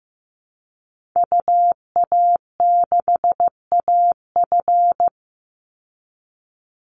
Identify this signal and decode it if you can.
Morse code, heard as a keyed tone, about 15 wpm, 700 Hz. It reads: UA6AF
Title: Call Signs Repeated 2x - 15wpm